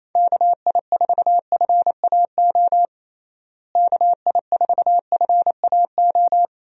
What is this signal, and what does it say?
Morse code, heard as a keyed tone, about 28 wpm, 700 Hz. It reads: KI4FAO KI4FAO
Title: USA Call Signs - 6 Characters - Repeated 3x - 28wpm